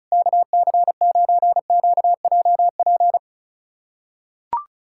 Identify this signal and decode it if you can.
Morse code, heard as a keyed tone, about 36 wpm, 700 Hz. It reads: KC9QJP
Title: USA Callsign DE Callsign - 6 Characters - 35wpm